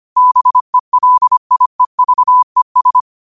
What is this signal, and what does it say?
Morse code, heard as a keyed tone, about 25 wpm, 1000 Hz. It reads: BELIEVES
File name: qotdfiles / 25wpm